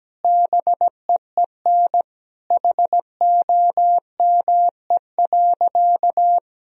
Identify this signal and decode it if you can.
Morse code, heard as a keyed tone, about 17 wpm, 700 Hz. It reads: BEEN HOME.
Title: Sentences from Top 1000 Words - Review Part 1 - 17wpm